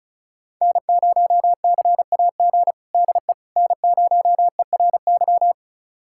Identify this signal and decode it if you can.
Morse code, heard as a keyed tone, about 36 wpm, 700 Hz. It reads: N0CAG DE N0ERY
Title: USA Callsign DE Callsign - 5 Characters - 35wpm